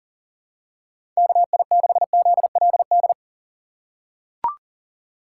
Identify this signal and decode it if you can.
Morse code, heard as a keyed tone, about 40 wpm, 700 Hz. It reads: KI6ZLD
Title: USA Call Signs - 3x Character Spacing - 40wpm